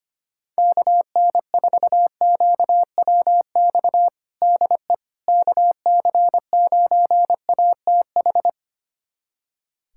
Morse code, heard as a keyed tone, about 25 wpm, 700 Hz. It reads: KN4QWX DE KC9ATH
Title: USA Callsign DE Callsign - 6 Characters - 25wpm